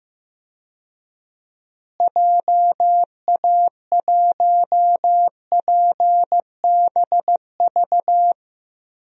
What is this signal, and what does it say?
Morse code, heard as a keyed tone, about 15 wpm, 700 Hz. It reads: JA1PBV
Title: DXCC-100 Call Signs - 6 Characters - Repeated 2x - 15wpm